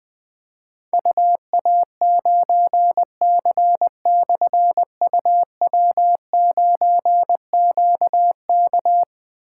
Morse code, heard as a keyed tone, about 20 wpm, 700 Hz. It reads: UA9C/UW9QK
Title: Difficult Call Signs 20wpm